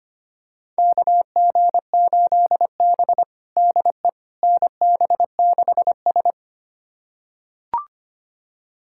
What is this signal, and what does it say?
Morse code, heard as a keyed tone, about 25 wpm, 700 Hz. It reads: KG8B DE NB6S
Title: USA Callsign DE Callsign - 4 Characters - 25wpm